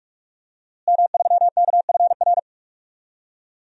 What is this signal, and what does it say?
Morse code, heard as a keyed tone, about 46 wpm, 700 Hz. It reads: M3KFR
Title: DXCC-100 Call Signs - 5 Characters - Repeated 2x - 45wpm